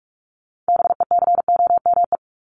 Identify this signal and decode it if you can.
Morse code, heard as a keyed tone, about 44 wpm, 700 Hz. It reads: BECOME
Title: Top 501-750 Words 45wpm